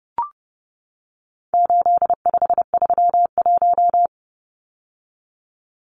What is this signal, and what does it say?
Morse code, heard as a keyed tone, about 30 wpm, 700 Hz. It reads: 8531
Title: Numbers - 4-Digit 30wpm